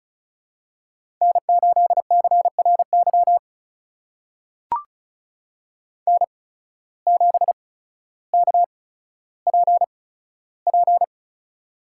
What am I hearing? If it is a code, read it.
Morse code, heard as a keyed tone, about 36 wpm, spaced as for 15 wpm, 700 Hz. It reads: N8CRY N7KPP
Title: USA Call Signs - 5 Characters - 8x Character Spacing - 35wpm